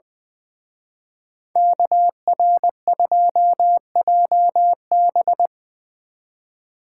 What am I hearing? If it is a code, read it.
Morse code, heard as a keyed tone, about 22 wpm, 700 Hz. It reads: KR2JB
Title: Speed Racing - USA 2x2 Call Signs 20wpm